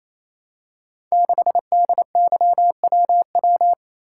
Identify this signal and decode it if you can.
Morse code, heard as a keyed tone, about 28 wpm, 700 Hz. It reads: 6DYWW
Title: Code Groups - 5 Letter:Numbers - 28wpm